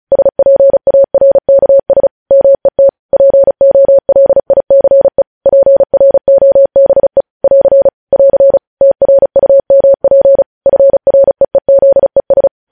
Morse code, heard as a keyed tone, about 36 wpm, 550 Hz. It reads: SPARKS MET POLICE PROBE <AR> <AR> TRUMP FREEZES
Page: news headlines in morse code at 35 wpm